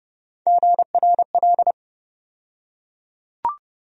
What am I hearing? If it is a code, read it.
Morse code, heard as a keyed tone, about 31 wpm, 700 Hz. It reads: GRL
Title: USA Call Sign Postfixes - 3 Letters 30wpm